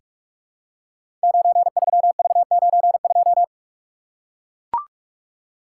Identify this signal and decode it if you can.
Morse code, heard as a keyed tone, about 44 wpm, 700 Hz. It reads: 93V92
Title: Code Groups - 5 Letter:Numbers - 45wpm